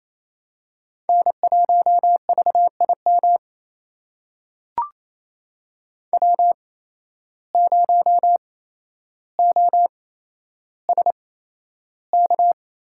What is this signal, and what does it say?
Morse code, heard as a keyed tone, about 28 wpm, spaced as for 12 wpm, 700 Hz. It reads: N1VIM W0OSK
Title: USA Call Signs - 5 Characters - 8x Character Spacing - 28wpm